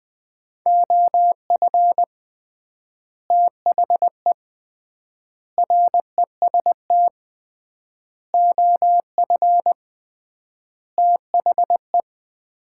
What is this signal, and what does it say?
Morse code, heard as a keyed tone, about 20 wpm, 700 Hz. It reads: OF THE REST OF THE
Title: 3x Word-Spacing - Sets of 5 Words - 20wpm